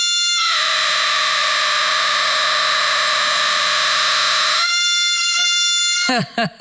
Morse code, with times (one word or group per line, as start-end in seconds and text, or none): none